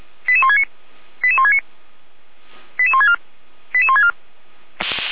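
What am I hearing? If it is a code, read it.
Five-tone selective call (ZVEI), heard as quick runs of five tones, tones of about 70 ms. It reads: 88169 88169 88164 88164
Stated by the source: ZVEI five-tone selective call received on SDR.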